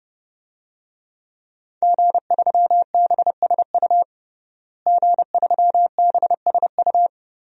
Morse code, heard as a keyed tone, about 30 wpm, 700 Hz. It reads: G3BSU G3BSU
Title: DXCC-100 Call Signs - 5 Characters - Repeated 3x - 30wpm